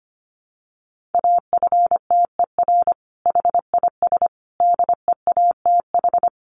Morse code, heard as a keyed tone, about 25 wpm, 700 Hz. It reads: AFTER HIS DEATH
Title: Sentences from Top 1100 Words 25wpm